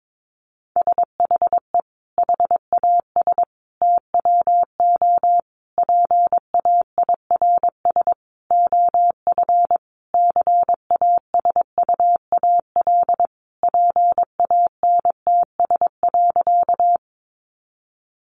Morse code, heard as a keyed tone, about 22 wpm, 700 Hz. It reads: SHE HAS TWO PAIRS OF CASUAL PANTS.
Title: Additional Easy Phrases 22wpm